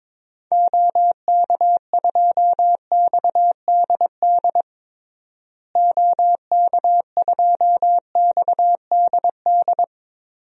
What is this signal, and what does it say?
Morse code, heard as a keyed tone, about 22 wpm, 700 Hz. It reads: OK2XDD OK2XDD
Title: DXCC-100 Call Signs - 6 Characters - Repeated 2x - 22wpm